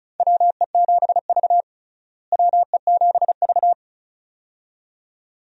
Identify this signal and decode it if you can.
Morse code, heard as a keyed tone, about 36 wpm, 700 Hz. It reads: WE7V WE7V
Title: USA Call Signs - 4 Characters - Repeated 2x - 35wpm